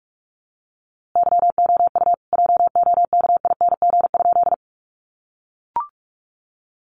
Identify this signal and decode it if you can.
Morse code, heard as a keyed tone, about 44 wpm, 700 Hz. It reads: YOU JOKING?
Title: A Sound Of Thunder 45wpm